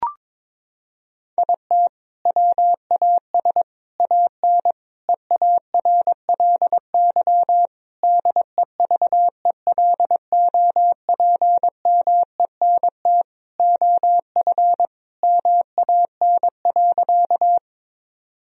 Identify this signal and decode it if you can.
Morse code, heard as a keyed tone, about 22 wpm, 700 Hz. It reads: IT WAS AN EARLY DEVELOPMENT OF MAN.
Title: Sentences from Top 1000 Words - Review Part 4 - 22wpm